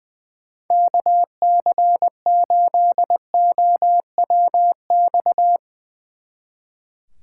Morse code, heard as a keyed tone, about 20 wpm, 700 Hz. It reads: KC8OWX
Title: Speed Racing - USA 2x3 Call Signs 20wpm